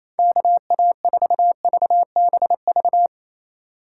Morse code, heard as a keyed tone, about 28 wpm, 700 Hz. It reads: KA4VBV
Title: USA Call Signs - 6 Characters - 3x Character Spacing - 28wpm